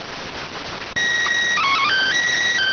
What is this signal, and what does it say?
SSTV (slow-scan television), PD 290.